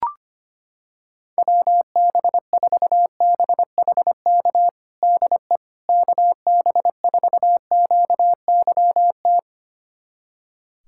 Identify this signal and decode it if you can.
Morse code, heard as a keyed tone, about 25 wpm, 700 Hz. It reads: WB4BHK DE KB4QYT
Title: USA Callsign DE Callsign - 6 Characters - 25wpm